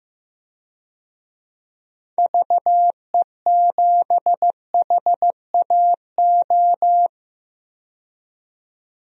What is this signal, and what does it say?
Morse code, heard as a keyed tone, about 15 wpm, 700 Hz. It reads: VE7HAO
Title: DXCC-100 Call Signs - 6 Characters - 15wpm